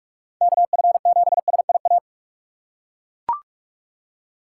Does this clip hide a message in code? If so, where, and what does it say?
Yes: Morse code from 0.4 to 2.0 s, heard as a keyed tone, about 42 wpm, 700 Hz. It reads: KF7SIA